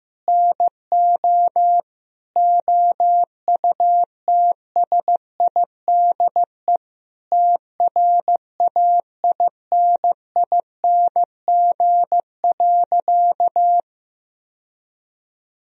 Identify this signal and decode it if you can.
Morse code, heard as a keyed tone, about 16 wpm, 700 Hz. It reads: NO OUTSIDE TRAINING.
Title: Sentences from Top 1000 Words - Review Part 4 - 15wpm